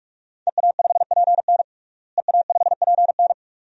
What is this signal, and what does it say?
Morse code, heard as a keyed tone, about 46 wpm, 700 Hz. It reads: EA5PN EA5PN
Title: DXCC-100 Call Signs - 5 Characters - Repeated 2x - 45wpm